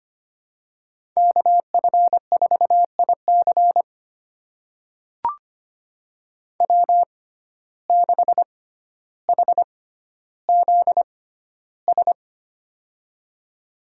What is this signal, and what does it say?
Morse code, heard as a keyed tone, about 25 wpm, spaced as for 13 wpm, 700 Hz. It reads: KF4IC W6HZS
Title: USA Call Signs - 5 Characters - 6x Character Spacing - 25wpm